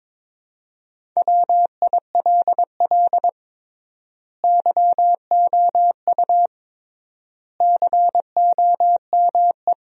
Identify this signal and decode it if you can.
Morse code, heard as a keyed tone, about 22 wpm, 700 Hz. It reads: WILL YOU COME
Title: 3x Word-Spacing - Sentences from Top 100 Words - 22wpm